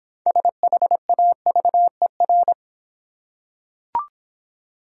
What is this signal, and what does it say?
Morse code, heard as a keyed tone, about 26 wpm, 700 Hz. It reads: SHAVER